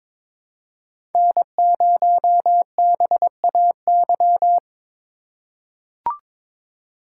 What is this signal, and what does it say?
Morse code, heard as a keyed tone, about 22 wpm, 700 Hz. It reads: N0BAY
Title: USA Call Signs - 5 Characters - 2x Character Spacing - 22wpm